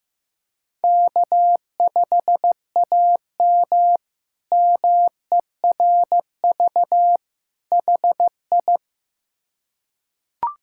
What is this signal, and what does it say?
Morse code, heard as a keyed tone, about 15 wpm, 700 Hz. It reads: K5AM MERV HI
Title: K1USN SST QSO Practice - 15wpm